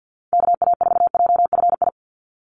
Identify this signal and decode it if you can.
Morse code, heard as a keyed tone, about 50 wpm, 700 Hz. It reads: KA4JFI
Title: USA Call Signs 50wpm